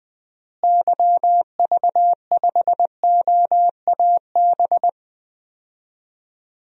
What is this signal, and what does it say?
Morse code, heard as a keyed tone, about 20 wpm, 700 Hz. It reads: YV5OAB
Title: DXCC-100 Call Signs - 6 Characters - Repeated 3x - 20wpm